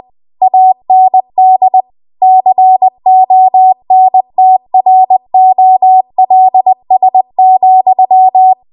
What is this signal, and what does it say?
Morse code, heard as a keyed tone, about 20 wpm, 750 Hz. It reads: AND CONTROLS,